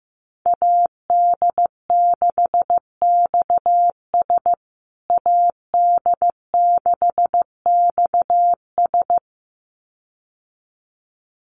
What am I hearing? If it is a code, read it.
Morse code, heard as a keyed tone, about 15 wpm, 700 Hz. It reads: AD6XS AD6XS
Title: USA Call Signs Repeated 2x 15wpm